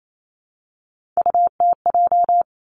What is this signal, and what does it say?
Morse code, heard as a keyed tone, about 28 wpm, 700 Hz. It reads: UTJ
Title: Three Challenging Letter-Number Combinations 28wpm